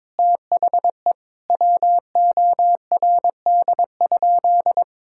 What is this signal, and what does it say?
Morse code, heard as a keyed tone, about 22 wpm, 700 Hz. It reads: THE WORD?
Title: Sentences from Top 1000 Words - Review Part 6 - 22wpm